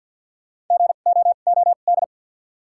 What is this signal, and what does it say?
Morse code, heard as a keyed tone, about 47 wpm, spaced as for 35 wpm, 700 Hz. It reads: GOOD
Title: 2x Character Spacing - Top 200 Words - 50wpm